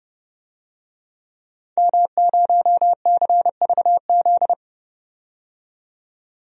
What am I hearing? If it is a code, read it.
Morse code, heard as a keyed tone, about 31 wpm, 700 Hz. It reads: M0CVZ
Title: DXCC-100 Call Signs - 5 Characters - 30wpm